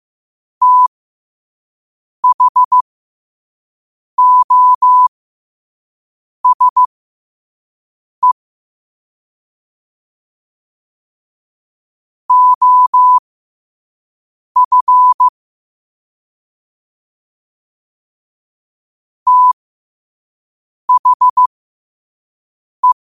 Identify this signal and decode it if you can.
Morse code, heard as a keyed tone, about 15 wpm, spaced as for 5 wpm, 1000 Hz. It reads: THOSE OF THE
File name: qotdfiles / 5wpm